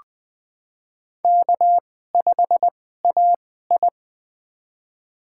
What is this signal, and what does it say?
Morse code, heard as a keyed tone, about 20 wpm, spaced as for 15 wpm, 700 Hz. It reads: K5AI
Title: USA Call Signs - 4 Characters - 2x Character Spacing - 20wpm